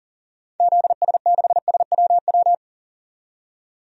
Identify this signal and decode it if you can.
Morse code, heard as a keyed tone, about 40 wpm, 700 Hz. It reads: ZS6SWW
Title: DXCC-100 Call Signs - 6 Characters - 40wpm